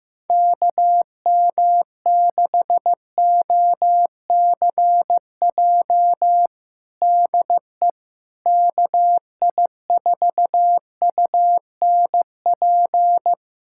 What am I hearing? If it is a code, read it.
Morse code, heard as a keyed tone, about 15 wpm, 700 Hz. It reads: KM6OCJ DE KI4UNP
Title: USA Callsign DE Callsign - 6 Characters - 15wpm